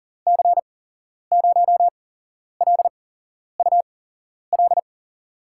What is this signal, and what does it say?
Morse code, heard as a keyed tone, about 40 wpm, spaced as for 11 wpm, 700 Hz. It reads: C0LUL
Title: USA Call Signs - 6 Characters - 8x Character Spacing - 40wpm